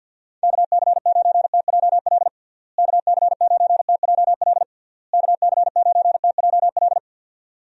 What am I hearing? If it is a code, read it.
Morse code, heard as a keyed tone, about 53 wpm, 700 Hz. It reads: KC9TJL KC9TJL KC9TJL
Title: USA Call Signs - 6 Characters - Repeated 3x - 50wpm